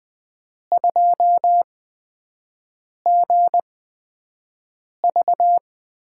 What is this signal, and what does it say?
Morse code, heard as a keyed tone, about 20 wpm, spaced as for 5 wpm, 700 Hz. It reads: 2GV
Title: USA Call Signs - 5 Characters - 8x Character Spacing - 20wpm